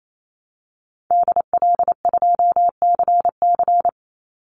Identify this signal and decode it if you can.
Morse code, heard as a keyed tone, about 28 wpm, 700 Hz. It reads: DL2CC
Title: Call Signs - 28wpm